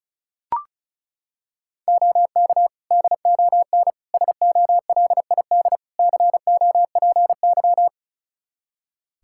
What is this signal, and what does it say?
Morse code, heard as a keyed tone, about 34 wpm, 700 Hz. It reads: OK DON SOLID COPY
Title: Practice QSO - Exchange 3 - More Info - 35wpm